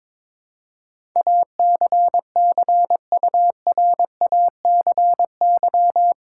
Morse code, heard as a keyed tone, about 22 wpm, 700 Hz. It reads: ACCURACY
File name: practice / 22wpm / Eight Letter Words - 22wpm